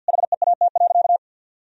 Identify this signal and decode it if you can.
Morse code, heard as a keyed tone, about 49 wpm, 700 Hz. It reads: HEAT.
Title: Sentences from Top 1000 Words - Review Part 3 - 50wpm